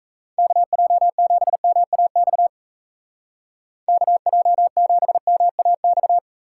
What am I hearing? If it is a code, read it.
Morse code, heard as a keyed tone, about 37 wpm, 700 Hz. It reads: KJ7MAX KJ7MAX